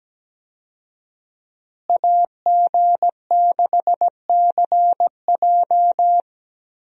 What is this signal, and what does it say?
Morse code, heard as a keyed tone, about 17 wpm, 700 Hz. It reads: AG6CJ
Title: USA Call Signs - 5 Characters - Repeated 3x - 17wpm